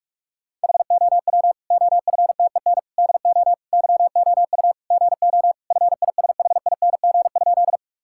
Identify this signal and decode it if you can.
Morse code, heard as a keyed tone, about 46 wpm, 700 Hz. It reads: HOW OFTEN DO YOU GO FISHING?
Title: Sentences from Top 1000 Words - Review Part 5 - 45wpm